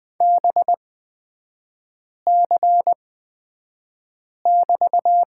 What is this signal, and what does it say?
Morse code, heard as a keyed tone, about 20 wpm, spaced as for 5 wpm, 700 Hz. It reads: BC<BT>
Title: 224 Lesson 36e - BT - Three Characters 20wpm